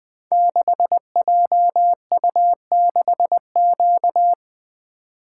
Morse code, heard as a keyed tone, about 20 wpm, 700 Hz. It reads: 6JU6Q
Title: Code Groups - 5 Letter:Numbers - 20wpm